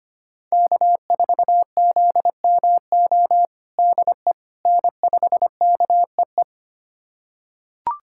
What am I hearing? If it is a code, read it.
Morse code, heard as a keyed tone, about 25 wpm, 700 Hz. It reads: K4ZMO DE N5KEE
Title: USA Callsign DE Callsign - 5 Characters - 25wpm